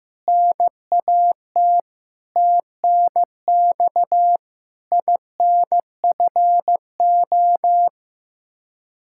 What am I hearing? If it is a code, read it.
Morse code, heard as a keyed tone, about 15 wpm, 700 Hz. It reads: NAT TNX INFO
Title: Practice QSO - Exchange 3 - More Info - 15wpm